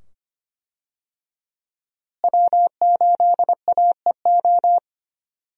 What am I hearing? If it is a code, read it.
Morse code, heard as a keyed tone, about 25 wpm, 700 Hz. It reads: W8AEO
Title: USA Call Signs - 5 Characters - Repeated 2x - 25wpm